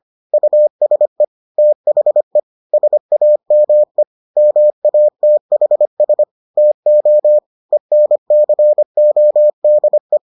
Morse code, heard as a keyed tone, about 25 wpm, 600 Hz. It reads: USE THE SAME MATHS TO ENCODE